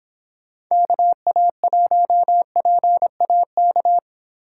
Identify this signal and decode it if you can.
Morse code, heard as a keyed tone, about 26 wpm, 700 Hz. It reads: KA1PAK